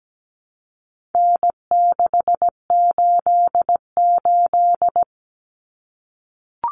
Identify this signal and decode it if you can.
Morse code, heard as a keyed tone, about 17 wpm, 700 Hz. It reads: N688
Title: Common Cut Numbers - 4-Digit 17wpm